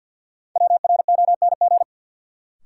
Morse code, heard as a keyed tone, about 53 wpm, 700 Hz. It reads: WRONG
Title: Sentences from Top 900 Words - New Word Drill - 50wpm